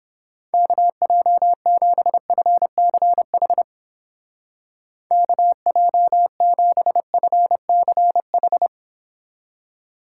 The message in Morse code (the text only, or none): KJ7FCH KJ7FCH